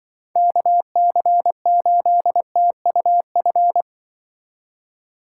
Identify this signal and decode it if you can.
Morse code, heard as a keyed tone, about 24 wpm, 700 Hz. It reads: KC8TUF
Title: Speed Racing - USA 2x3 Call Signs 20wpm